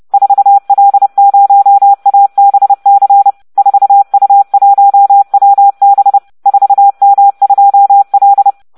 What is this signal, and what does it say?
Morse code, heard as a keyed tone, about 30 wpm, 800 Hz. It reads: 4L0ABC 4U1WB 4M2L